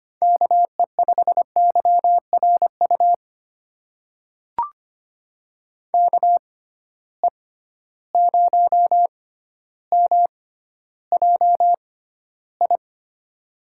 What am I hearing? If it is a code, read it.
Morse code, heard as a keyed tone, about 25 wpm, spaced as for 13 wpm, 700 Hz. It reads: KE5YRU KE0MJI
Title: USA Call Signs - 6 Characters - 6x Character Spacing - 25wpm